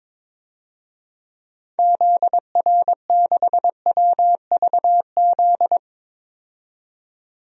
Morse code, heard as a keyed tone, about 22 wpm, 700 Hz. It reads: ZR6WVZ
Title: DXCC-100 Call Signs - 6 Characters - 22wpm